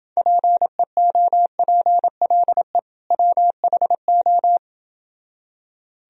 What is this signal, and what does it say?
Morse code, heard as a keyed tone, about 29 wpm, 700 Hz. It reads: PEOPLE WHO